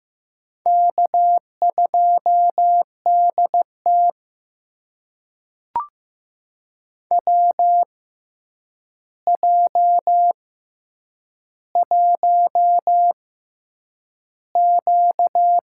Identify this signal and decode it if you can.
Morse code, heard as a keyed tone, about 15 wpm, spaced as for 8 wpm, 700 Hz. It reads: K2DT WJ1Q
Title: USA Call Signs - 4 Characters - 6x Character Spacing - 15wpm